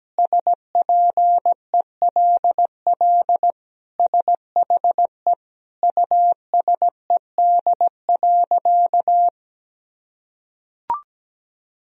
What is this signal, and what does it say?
Morse code, heard as a keyed tone, about 17 wpm, 700 Hz. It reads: SPELL SHE USED.
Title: Sentences from Top 1000 Words - Review Part 1 - 17wpm